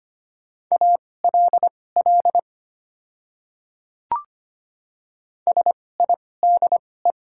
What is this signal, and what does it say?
Morse code, heard as a keyed tone, about 25 wpm, spaced as for 18 wpm, 690 Hz. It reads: ALL SIDE
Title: 2x Character Spacing - Top 100 Words - 25wpm